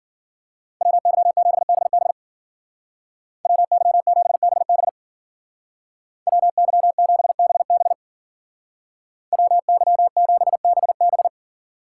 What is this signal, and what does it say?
Morse code, heard as a keyed tone, about 48 wpm, 700 Hz. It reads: WY7BB WY7BB WY7BB WY7BB
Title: Speed Racing - USA 2x2 Call Signs 40wpm